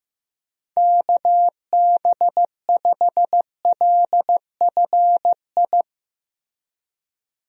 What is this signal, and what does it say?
Morse code, heard as a keyed tone, about 16 wpm, 700 Hz. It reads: KB5LFI
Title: USA Call Signs - 6 Characters - Repeated 3x - 15wpm